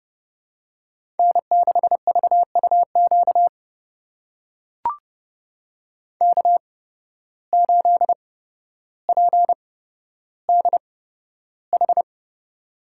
Morse code, heard as a keyed tone, about 31 wpm, spaced as for 13 wpm, 700 Hz. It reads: N6VUQ K8PDH
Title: USA Call Signs - 5 Characters - 8x Character Spacing - 30wpm